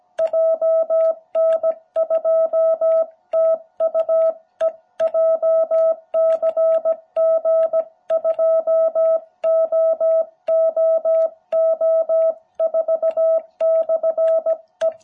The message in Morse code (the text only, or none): JN2TUE JCG2OOO4/E